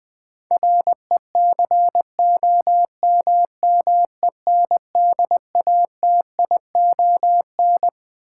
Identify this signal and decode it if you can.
Morse code, heard as a keyed tone, about 21 wpm, 700 Hz. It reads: RECOMMENDATION
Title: Sentences from Top 1400 Words - New Word Drill - 20wpm